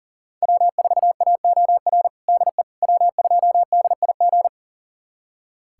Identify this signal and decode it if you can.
Morse code, heard as a keyed tone, about 37 wpm, 700 Hz. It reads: W4AOR DE W2DIG
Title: USA Callsign DE Callsign - 5 Characters - 40wpm